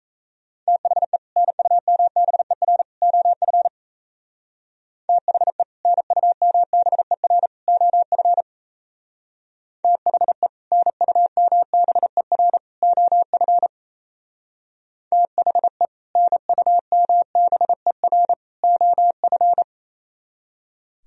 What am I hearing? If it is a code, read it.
Morse code, heard as a keyed tone, about 35 wpm, 700 Hz. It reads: THE NUMBER OF THE NUMBER OF THE NUMBER OF THE NUMBER OF
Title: Speed-Racing - Sets of 3 Words 28wpm